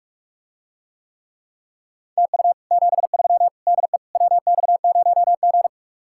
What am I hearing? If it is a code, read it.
Morse code, heard as a keyed tone, about 49 wpm, 700 Hz. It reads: TU 73 DE WX0G